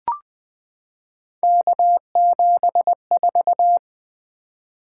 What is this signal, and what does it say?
Morse code, heard as a keyed tone, about 20 wpm, 700 Hz. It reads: K74
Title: Three Letter-Number Combinations - 20wpm